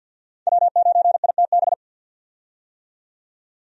Morse code, heard as a keyed tone, about 47 wpm, 700 Hz. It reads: W9ITB
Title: USA Call Signs - 5 Characters - 50wpm